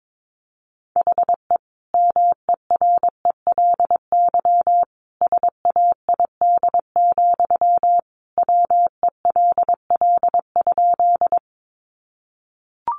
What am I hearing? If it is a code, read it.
Morse code, heard as a keyed tone, about 22 wpm, 700 Hz. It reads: HE MERELY SAID, WELL?
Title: Sentences from Top 1400 Words 22wpm